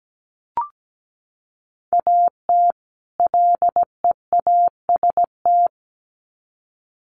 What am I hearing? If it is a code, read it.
Morse code, heard as a keyed tone, about 17 wpm, 700 Hz. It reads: AT LEAST
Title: Sets of 2 Words 17wpm